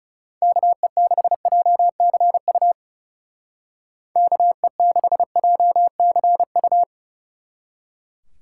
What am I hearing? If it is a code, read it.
Morse code, heard as a keyed tone, about 40 wpm, 700 Hz. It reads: KE6JCU KE6JCU